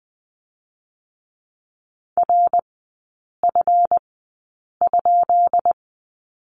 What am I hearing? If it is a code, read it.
Morse code, heard as a keyed tone, about 20 wpm, spaced as for 8 wpm, 700 Hz. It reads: RF?
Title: 138 Lesson 23e - Letter f - Three Characters 20wpm